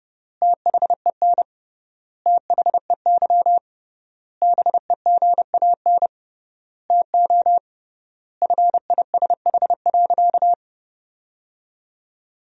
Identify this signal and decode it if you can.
Morse code, heard as a keyed tone, about 30 wpm, 700 Hz. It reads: THEN THEY BEGAN TO FISH.
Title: 3x Word-Spacing - Sentences from Top 300 Words - 30wpm